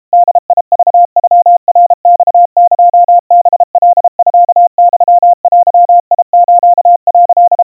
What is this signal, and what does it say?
Morse code (Japanese Wabun code), heard as a keyed tone, about 32 wpm, 700 Hz. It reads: ボクノナマエハカミユデス」
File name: kawa / Morse sample / kamyu 120 cpm 700 Hz Crispy